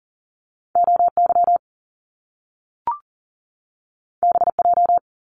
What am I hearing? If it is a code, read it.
Morse code, heard as a keyed tone, about 40 wpm, 700 Hz. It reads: OY BJ
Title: Top Two Letter Combinations in Words 40wpm